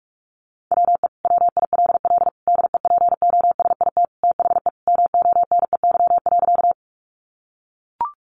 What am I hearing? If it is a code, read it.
Morse code, heard as a keyed tone, about 44 wpm, 700 Hz. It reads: WE WILL DEPOSIT THE MONEY.
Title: Sentences from Top 1300 Words 45wpm